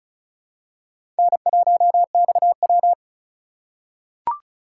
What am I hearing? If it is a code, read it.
Morse code, heard as a keyed tone, about 36 wpm, 700 Hz. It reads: N1XW
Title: USA Call Signs - 4 Characters - 4x Character Spacing - 35wpm